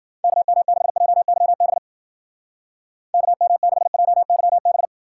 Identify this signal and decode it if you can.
Morse code, heard as a keyed tone, about 50 wpm, 700 Hz. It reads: KM6JYB KM6JYB
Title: Speed Racing - USA 2x3 Call Signs 40wpm